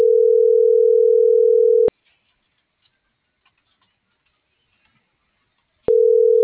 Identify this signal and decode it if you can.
Ringback tone (North America), 440 + 480 Hz.